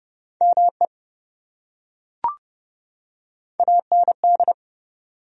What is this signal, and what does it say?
Morse code, heard as a keyed tone, about 27 wpm, 700 Hz. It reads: ME AND